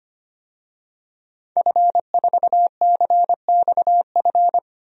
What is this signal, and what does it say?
Morse code, heard as a keyed tone, about 25 wpm, 700 Hz. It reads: F4CXF
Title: DXCC-100 Call Signs - 5 Characters - Repeated 3x - 25wpm